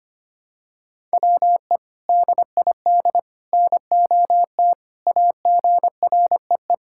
Morse code, heard as a keyed tone, about 25 wpm, 700 Hz. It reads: WE DID NOT AGREE
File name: practice / 25wpm / Sentences with 3-Word Phrases - 25wpm